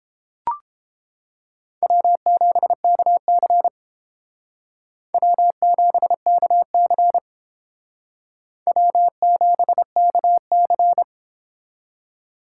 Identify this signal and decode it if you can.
Morse code, heard as a keyed tone, about 29 wpm, 700 Hz. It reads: W7KC W7KC W7KC